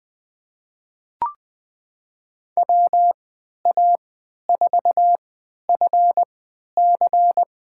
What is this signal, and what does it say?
Morse code, heard as a keyed tone, about 21 wpm, spaced as for 11 wpm, 700 Hz. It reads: WA4FC
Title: USA Call Signs - 3x Character Spacing - 20wpm